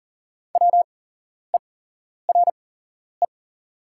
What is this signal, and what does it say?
Morse code, heard as a keyed tone, about 40 wpm, spaced as for 11 wpm, 690 Hz. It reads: WERE
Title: 8x Character Spacing - Top 100 Words - 40wpm